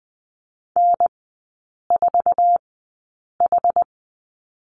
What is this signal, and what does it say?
Morse code, heard as a keyed tone, about 20 wpm, spaced as for 8 wpm, 700 Hz. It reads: N4H
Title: 067 Lesson 12e - Letter d - Three Characters 20wpm